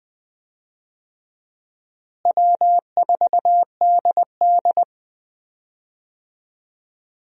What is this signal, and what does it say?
Morse code, heard as a keyed tone, about 20 wpm, 700 Hz. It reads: W4DD
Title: USA Call Signs - 4 Characters - Repeated 3x - 20wpm